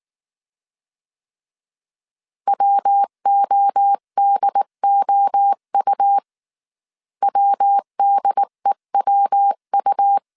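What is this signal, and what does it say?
Morse code, heard as a keyed tone, about 19 wpm, 800 Hz. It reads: WODOU WDEWU
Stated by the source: Morse code dictation at 18 wpm from the IW3SGT QRS Morse code training course, five-character groups of 300 characters.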